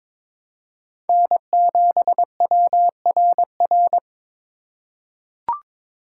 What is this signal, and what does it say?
Morse code, heard as a keyed tone, about 22 wpm, 700 Hz. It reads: N7WRR
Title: USA Call Signs - 5 Characters - 3x Character Spacing - 22wpm